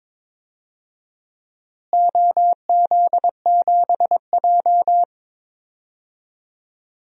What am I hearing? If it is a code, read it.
Morse code, heard as a keyed tone, about 22 wpm, 700 Hz. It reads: OZ7J
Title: DXCC-100 Call Signs - 4 Characters - Repeated 2x - 22wpm